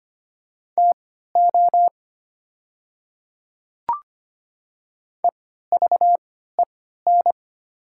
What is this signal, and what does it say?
Morse code, heard as a keyed tone, about 25 wpm, spaced as for 14 wpm, 700 Hz. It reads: TO EVEN
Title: 3x Character Spacing - Top 200 Words - 25wpm